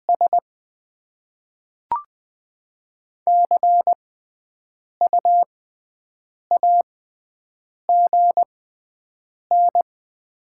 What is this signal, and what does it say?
Morse code, heard as a keyed tone, about 20 wpm, spaced as for 7 wpm, 700 Hz. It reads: S CU AGN
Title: Top 20 QSO Elements - 6x Character Spacing - 20wpm